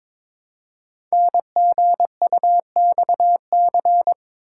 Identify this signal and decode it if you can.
Morse code, heard as a keyed tone, about 22 wpm, 700 Hz. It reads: NGUXC
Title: Code Groups - 5 Letter:Numbers - 22wpm